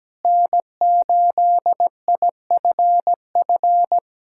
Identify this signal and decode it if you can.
Morse code, heard as a keyed tone, about 17 wpm, 700 Hz. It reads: N8IFF
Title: USA Callsign DE Callsign - 5 Characters - 17wpm